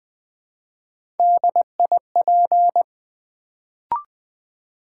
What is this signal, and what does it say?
Morse code, heard as a keyed tone, about 20 wpm, 700 Hz. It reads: DIP
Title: Build A Word - 1 Letter At A Time - 3-Letter Words - 20wpm